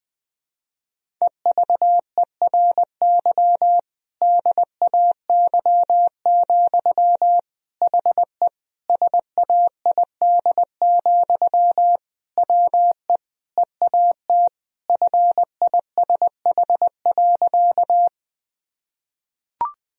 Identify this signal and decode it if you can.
Morse code, heard as a keyed tone, about 20 wpm, 700 Hz. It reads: EVERY DAY, HE SAID, WE EAT FISH.
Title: Sentences from Top 1000 Words - Review Part 1 - 20wpm